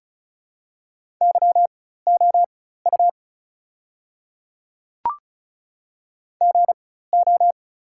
Morse code, heard as a keyed tone, about 35 wpm, spaced as for 16 wpm, 700 Hz. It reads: YOU GO
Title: 4x Character Spacing - Top 200 Words - 35wpm